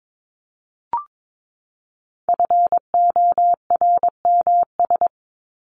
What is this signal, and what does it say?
Morse code, heard as a keyed tone, about 22 wpm, 700 Hz. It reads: FORMS
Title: ES:S Suffix - Top 100 Words with Base Words 22wpm